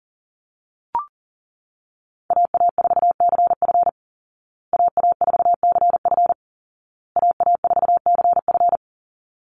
Morse code, heard as a keyed tone, about 40 wpm, 700 Hz. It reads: AA4CF AA4CF AA4CF
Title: Call Signs Repeated 2x - 40wpm